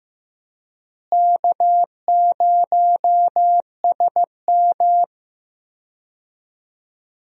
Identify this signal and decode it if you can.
Morse code, heard as a keyed tone, about 15 wpm, 700 Hz. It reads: K0SM
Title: USA Call Signs - 4 Characters - Repeated 2x - 15wpm